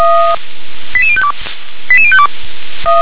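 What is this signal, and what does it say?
Five-tone selective call (ZVEI), heard as quick runs of five tones, tones of about 70 ms. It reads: 88052 88052
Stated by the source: ZVEI five-tone call received by SDR.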